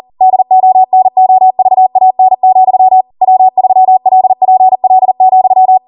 Morse code, heard as a keyed tone, about 40 wpm, 750 Hz. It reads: DONOVAN, W3LPL,